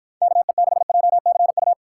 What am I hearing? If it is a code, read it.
Morse code, heard as a keyed tone, about 53 wpm, 700 Hz. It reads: KE6JCU